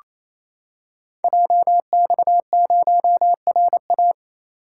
Morse code, heard as a keyed tone, about 29 wpm, 700 Hz. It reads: JX0RA
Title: Code Groups - 5 Letter:Numbers - 28wpm